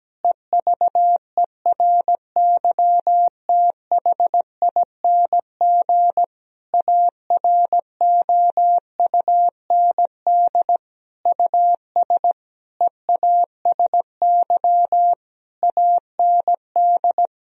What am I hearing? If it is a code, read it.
Morse code, heard as a keyed tone, about 17 wpm, 700 Hz. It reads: EVERYTHING AROUND US EASY AND
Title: Sentences from Top 1000 Words - Review Part 3 - 17wpm